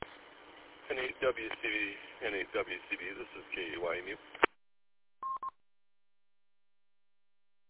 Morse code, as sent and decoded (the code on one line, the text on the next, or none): none